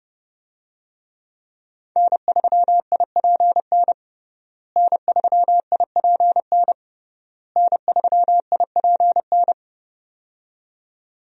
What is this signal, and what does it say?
Morse code, heard as a keyed tone, about 30 wpm, 700 Hz. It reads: N3IPN N3IPN N3IPN
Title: USA Call Signs - 5 Characters - Repeated 3x - 30wpm